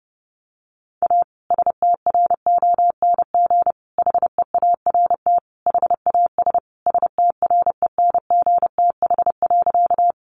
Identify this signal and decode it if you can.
Morse code, heard as a keyed tone, about 30 wpm, 700 Hz. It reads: A STRONG HEART HAS STRENGTH.
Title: Sentences from Top 1300 Words 30wpm